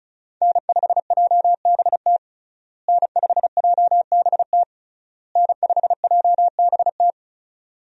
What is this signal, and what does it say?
Morse code, heard as a keyed tone, about 35 wpm, 700 Hz. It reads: N5JBT N5JBT N5JBT
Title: USA Call Signs - 5 Characters - Repeated 3x - 35wpm